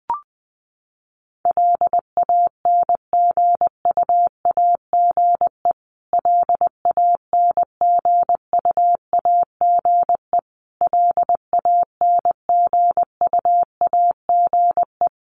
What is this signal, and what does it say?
Morse code, heard as a keyed tone, about 20 wpm, 700 Hz. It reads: LANGUAGE LANGUAGE LANGUAGE
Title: Top 500 Words Repeated 20wpm